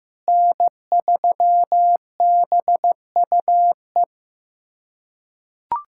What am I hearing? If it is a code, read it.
Morse code, heard as a keyed tone, about 15 wpm, 700 Hz. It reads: N3BUE
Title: USA Callsign DE Callsign - 5 Characters - 15wpm